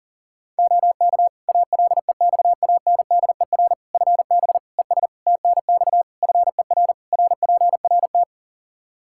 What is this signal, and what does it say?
Morse code, heard as a keyed tone, about 40 wpm, 700 Hz. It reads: OK ALEXANDER FB ES TNX FER RPRT
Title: Practice QSO - Exchange 2 - Info - 40wpm